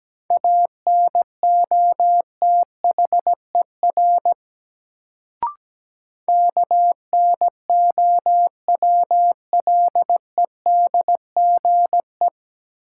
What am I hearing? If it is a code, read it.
Morse code, heard as a keyed tone, about 17 wpm, 700 Hz. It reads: ANOTHER KNOWLEDGE
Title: Sentences from Top 600 Words - New Word Drill - 17wpm